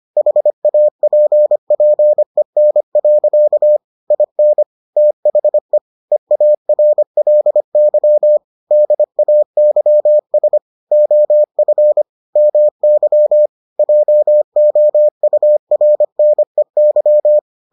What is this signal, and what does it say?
Morse code, heard as a keyed tone, about 25 wpm, 600 Hz. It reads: HAPPEN. IN THE EARLY DAYS OF MY JOURNEY